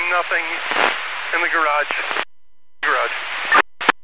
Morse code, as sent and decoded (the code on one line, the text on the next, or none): none